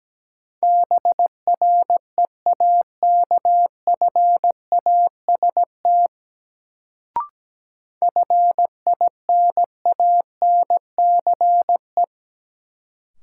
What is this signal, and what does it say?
Morse code, heard as a keyed tone, about 17 wpm, 700 Hz. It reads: BREAKFAST FINANCE
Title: Sentences from Top 1500 Words - New Word Drill Alt - 17wpm